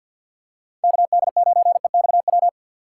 Morse code, heard as a keyed tone, about 52 wpm, 700 Hz. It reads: KD9EXW
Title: USA Call Signs - 6 Characters - 3x Character Spacing - 50wpm